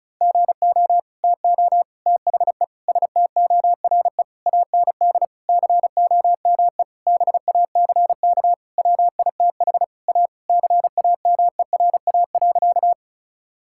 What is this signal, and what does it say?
Morse code, heard as a keyed tone, about 36 wpm, 700 Hz. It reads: GO TO THE STORE AND COME BACK WITH A CAMERA.
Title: Sentences from Top 1000 Words - Review Part 5 - 35wpm